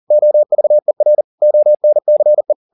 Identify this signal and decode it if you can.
Morse code, heard as a keyed tone, about 40 wpm, 600 Hz. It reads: OVER ONCE
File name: the old man and the sea 2 40wpm